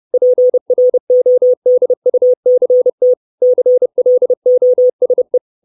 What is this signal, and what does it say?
Morse code, heard as a keyed tone, about 30 wpm, 500 Hz. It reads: PRODUCT CLOSE
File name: 632 Words 30WPM